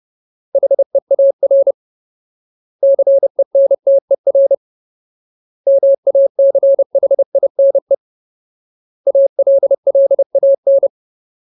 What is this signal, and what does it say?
Morse code, heard as a keyed tone, about 30 wpm, 550 Hz. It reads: HEAR CENTER MACHINE ALLAN
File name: Ultimate Word List / 30WPM Ultimate